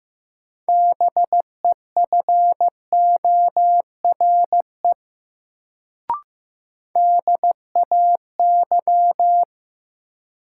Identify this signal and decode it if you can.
Morse code, heard as a keyed tone, about 15 wpm, 700 Hz. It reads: BEFORE DAY
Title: Sentences from Top 200 Words - New Word Drill - 15wpm